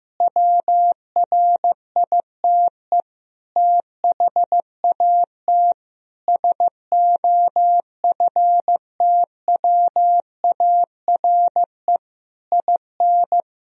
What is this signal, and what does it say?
Morse code, heard as a keyed tone, about 15 wpm, 700 Hz. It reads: WRITE THAT SOFTWARE IN
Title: Sentences from Top 1000 Words - Review Part 3 - 15wpm